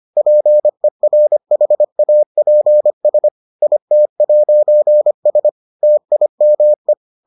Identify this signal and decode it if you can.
Morse code, heard as a keyed tone, about 25 wpm, 600 Hz. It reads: PERHAPS IT'S TIME